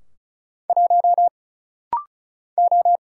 Morse code, heard as a keyed tone, about 36 wpm, 700 Hz. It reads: 1 O
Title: Single Letter-Number - Rapid-Fire 35wpm